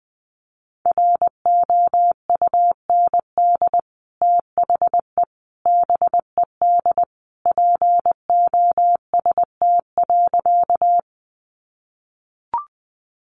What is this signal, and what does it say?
Morse code, heard as a keyed tone, about 20 wpm, 700 Hz. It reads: ROUND THE BED POST.
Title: Sentences from Top 1000 Words 20wpm